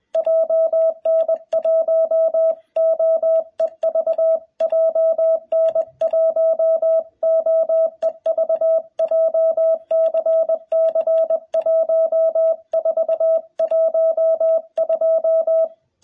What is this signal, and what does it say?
Morse code, heard as a keyed tone, about 20 wpm, 660 Hz. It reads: JN1OEV JN1OEV JCC1412
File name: jh8cbh / 2021cw